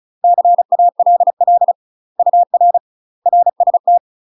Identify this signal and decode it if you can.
Morse code, heard as a keyed tone, about 34 wpm, 700 Hz. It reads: CALL UR RST
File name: UK MASTER QSOs TRAINING / 35WPM / UK MASTER QSOs 0015